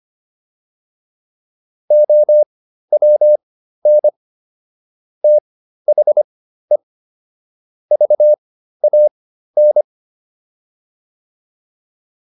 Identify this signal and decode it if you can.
Morse code, heard as a keyed tone, about 25 wpm, spaced as for 13 wpm, 600 Hz. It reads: OWN THE VAN